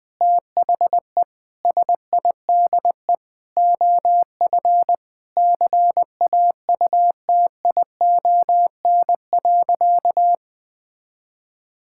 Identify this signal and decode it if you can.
Morse code, heard as a keyed tone, about 20 wpm, 700 Hz. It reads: THE SIDE OF CAUTION.
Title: Sentences with 7-Word Phrases - 20wpm